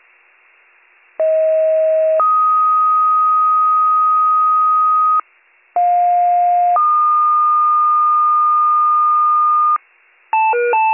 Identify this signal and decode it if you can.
Two-tone page, heard as a one-second tone, then a three-second tone, about 640 then 1250 Hz, about 710 then 1220 Hz. It is heard from 1.2 to 9.8 s.